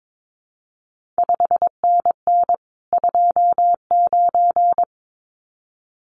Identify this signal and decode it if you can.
Morse code, heard as a keyed tone, about 22 wpm, 700 Hz. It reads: 5NN 29
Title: CQ WW DX Contest Exchange Encore - 22wpm